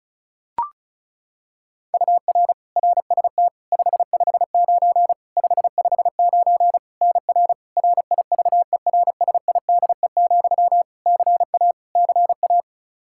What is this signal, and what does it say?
Morse code, heard as a keyed tone, about 34 wpm, 700 Hz. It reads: UR RST 559 559 NR RIVERSIDE, CA CA
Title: Practice QSO - Exchange 1 - Beginning - 35wpm